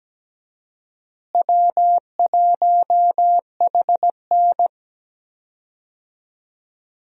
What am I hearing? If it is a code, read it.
Morse code, heard as a keyed tone, about 17 wpm, 700 Hz. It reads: W1HN
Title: USA Call Signs - 4 Characters - Repeated 3x - 17wpm